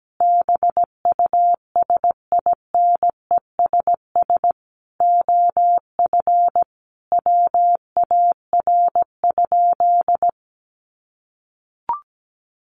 Morse code, heard as a keyed tone, about 17 wpm, 700 Hz. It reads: BUSINESS OF WAR?
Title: Sentences from Top 1300 Words 17wpm